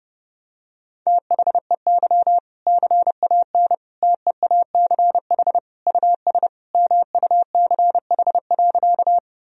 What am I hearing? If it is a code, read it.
Morse code, heard as a keyed tone, about 32 wpm, 700 Hz. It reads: THEY CAN TEACH US MUCH.
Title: Sentences from Top 1000 Words - Review Part 2 - 30wpm